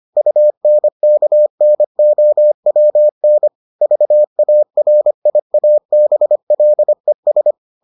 Morse code, heard as a keyed tone, about 25 wpm, 600 Hz. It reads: UNKNOWN VARIABLES